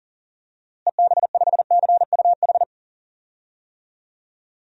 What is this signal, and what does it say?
Morse code, heard as a keyed tone, about 40 wpm, 700 Hz. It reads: EB5CUH
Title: DXCC-100 Call Signs - 6 Characters - Repeated 3x - 40wpm